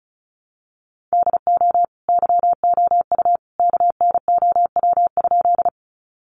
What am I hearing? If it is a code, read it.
Morse code, heard as a keyed tone, about 34 wpm, 700 Hz. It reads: DO YOU KNOW?
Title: Sentences from Top 100 Words 35wpm